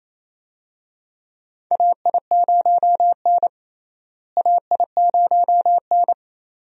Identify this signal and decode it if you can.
Morse code, heard as a keyed tone, about 28 wpm, 700 Hz. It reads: AI0N AI0N
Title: USA Call Signs - 4 Characters - Repeated 2x - 28wpm